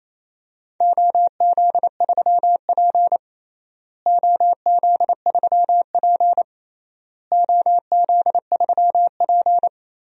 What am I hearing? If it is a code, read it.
Morse code, heard as a keyed tone, about 28 wpm, 700 Hz. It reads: OZ3P OZ3P OZ3P
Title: DXCC-100 Call Signs - 4 Characters - Repeated 3x - 28wpm